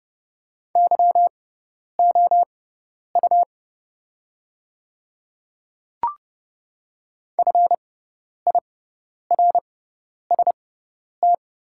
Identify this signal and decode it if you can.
Morse code, heard as a keyed tone, about 30 wpm, spaced as for 10 wpm, 700 Hz. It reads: YOU FIRST
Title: 6x Character Spacing - Top 100 Words - 30wpm